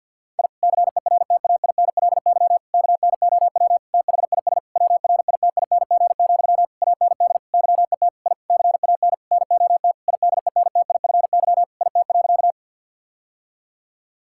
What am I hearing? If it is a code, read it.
Morse code, heard as a keyed tone, about 50 wpm, 700 Hz. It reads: I CERTAINLY KNOW THIS WRITING, AND YET I CAN NOT IDENTIFY IT.
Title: Sentences from Top 1000 Words - Review Part 6 - 50wpm